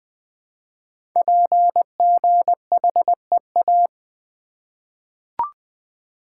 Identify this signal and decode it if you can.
Morse code, heard as a keyed tone, about 21 wpm, 700 Hz. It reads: PGHEA
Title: Code Groups - 5 Letters - 20wpm